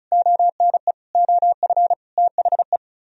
Morse code, heard as a keyed tone, about 45 wpm, 700 Hz. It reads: ONE OF THE